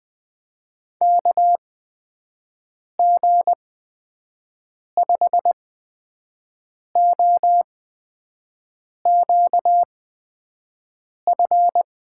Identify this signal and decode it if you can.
Morse code, heard as a keyed tone, about 20 wpm, spaced as for 5 wpm, 700 Hz. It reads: KG5OQF
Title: USA Call Signs - 6 Characters - 8x Character Spacing - 20wpm